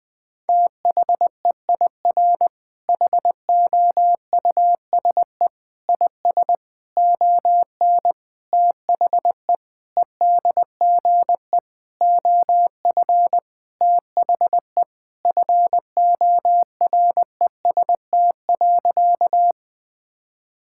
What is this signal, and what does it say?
Morse code, heard as a keyed tone, about 21 wpm, 700 Hz. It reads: THEIR HOUSE IS ON THE EDGE OF THE FOREST.
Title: Sentences with 5-Word Phrases - 20wpm